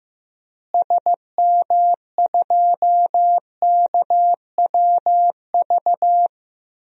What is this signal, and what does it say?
Morse code, heard as a keyed tone, about 15 wpm, 700 Hz. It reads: SM2KWV
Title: DXCC-100 Call Signs - 6 Characters - Repeated 2x - 15wpm